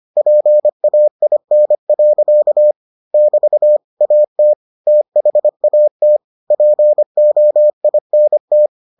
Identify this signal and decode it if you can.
Morse code, heard as a keyed tone, about 25 wpm, 600 Hz. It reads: PAIN. <BT> AT THAT POINT